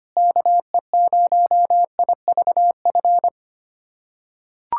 Morse code, heard as a keyed tone, about 25 wpm, 700 Hz. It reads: KE0IVF
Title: USA Callsign DE Callsign - 6 Characters - 25wpm